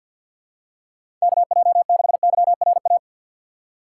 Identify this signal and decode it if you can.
Morse code, heard as a keyed tone, about 47 wpm, 700 Hz. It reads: KJ6YRA
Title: USA Call Signs - 2x Character Spacing - 50wpm